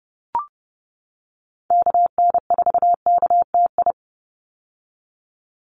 Morse code, heard as a keyed tone, about 30 wpm, 700 Hz. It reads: KN4KTI
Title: USA Call Signs 30wpm